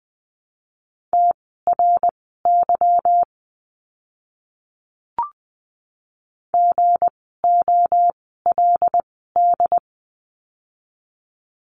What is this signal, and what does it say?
Morse code, heard as a keyed tone, about 20 wpm, spaced as for 15 wpm, 700 Hz. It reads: TRY GOLD
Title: 160 Lesson 26f - Letter g - Words 20wpm